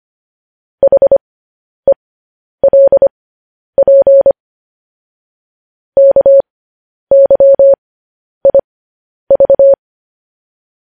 Morse code, heard as a keyed tone, about 25 wpm, spaced as for 10 wpm, 550 Hz. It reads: HELP KYIV
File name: podcast / News headlines-1x-25x10